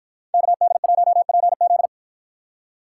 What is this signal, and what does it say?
Morse code, heard as a keyed tone, about 49 wpm, 700 Hz. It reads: KD1PZ